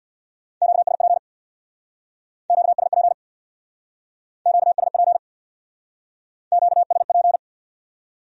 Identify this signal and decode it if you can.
Morse code, heard as a keyed tone, about 64 wpm, 700 Hz. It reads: QSP QSP QSP QSP